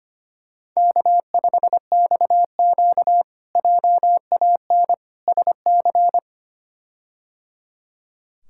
Morse code, heard as a keyed tone, about 25 wpm, 700 Hz. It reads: K5XQ JAN SC
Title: K1USN SST QSO Practice - 25wpm